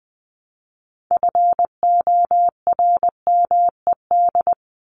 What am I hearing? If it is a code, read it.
Morse code, heard as a keyed tone, about 20 wpm, 700 Hz. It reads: FORMED
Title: ED Suffix - Top 100 Words with Base Words 20wpm